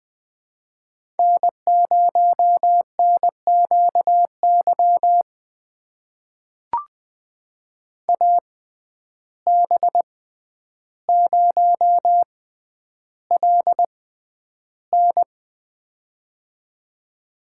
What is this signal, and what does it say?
Morse code, heard as a keyed tone, about 20 wpm, spaced as for 10 wpm, 700 Hz. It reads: N0NQY AB0LN
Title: USA Call Signs - 5 Characters - 6x Character Spacing - 20wpm